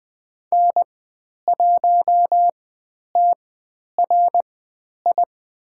Morse code, heard as a keyed tone, about 20 wpm, spaced as for 10 wpm, 700 Hz. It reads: N1TRI
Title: 109 Lesson 18g - Letter m - Call Signs 20wpm